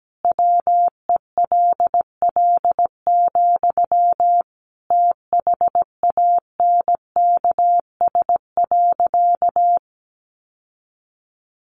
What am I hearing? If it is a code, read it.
Morse code, heard as a keyed tone, about 17 wpm, 700 Hz. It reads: WELL, THANKS.
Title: Sentences from Top 700 Words 17wpm